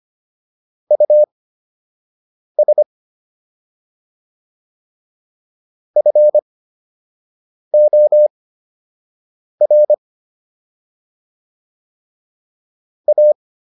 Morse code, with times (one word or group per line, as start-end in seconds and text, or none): 0.9-2.8 US
6.0-9.9 FOR
13.1-13.3 A